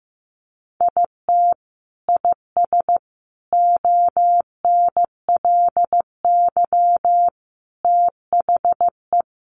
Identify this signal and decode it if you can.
Morse code, heard as a keyed tone, about 15 wpm, 700 Hz. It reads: IT IS ONLY THE
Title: Sentences from Top 800 Words 15wpm